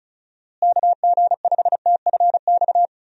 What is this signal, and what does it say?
Morse code, heard as a keyed tone, about 36 wpm, 700 Hz. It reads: KG5TFX
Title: USA Call Signs - 6 Characters - Repeated 2x - 35wpm